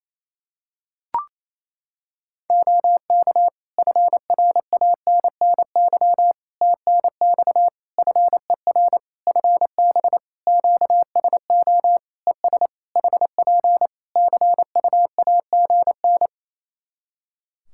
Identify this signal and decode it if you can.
Morse code, heard as a keyed tone, about 28 wpm, 700 Hz. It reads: OK FRANNY TNX FER FB QSO ES HP CUAGN
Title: Practice QSO - Exchange 4 - Ending - 28wpm